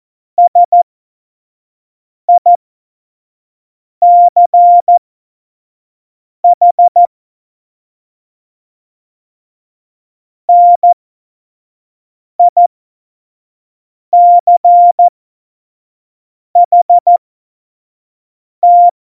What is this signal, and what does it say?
Morse code, heard as a keyed tone, about 14 wpm, spaced as for 5 wpm, 700 Hz. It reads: SICH NICHT
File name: Morsekurs / Witze / Witz 10. Oktober 17, Tempo 25